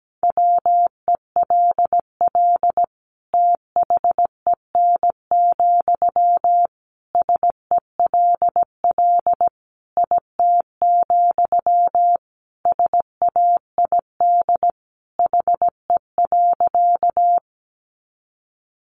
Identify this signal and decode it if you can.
Morse code, heard as a keyed tone, about 17 wpm, 700 Hz. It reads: WELL THEN, SELL IT, SAID HE.
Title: Sentences from Top 1100 Words 17wpm